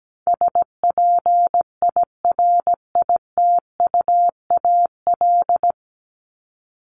Morse code, heard as a keyed tone, about 17 wpm, 700 Hz. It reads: SPIRITUAL
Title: Top 1501-1750 Words 17wpm